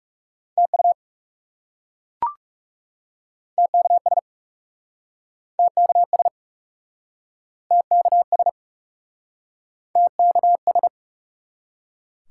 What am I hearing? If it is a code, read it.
Morse code, heard as a keyed tone, about 42 wpm, 700 Hz. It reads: TU TKS TKS TKS TKS